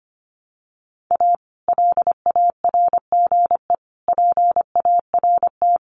Morse code, heard as a keyed tone, about 25 wpm, 700 Hz. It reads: A LARGE PART
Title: Sets of 5 Words Encore2 25wpm